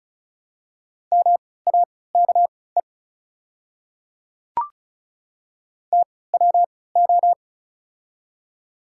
Morse code, heard as a keyed tone, about 36 wpm, spaced as for 20 wpm, 690 Hz. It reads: MAKE TWO
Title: 3x Character Spacing - Top 100 Words - 35wpm